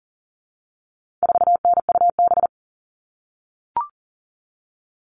Morse code, heard as a keyed tone, about 40 wpm, 700 Hz. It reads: 4NUB
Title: All Cut Numbers - 4-Digit 40wpm